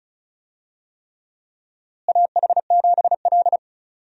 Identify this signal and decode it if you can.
Morse code, heard as a keyed tone, about 36 wpm, 700 Hz. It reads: AH7L
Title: USA Call Signs - 4 Characters - Repeated 3x - 35wpm